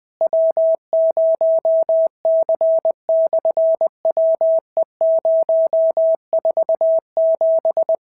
Morse code, heard as a keyed tone, about 20 wpm, 650 Hz. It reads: W0C/WE047
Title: SOTA QSO Practice 20wpm